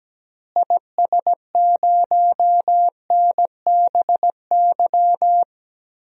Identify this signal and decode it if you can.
Morse code, heard as a keyed tone, about 17 wpm, 700 Hz. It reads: IS0NBY
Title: DXCC-100 Call Signs - 6 Characters - Repeated 3x - 17wpm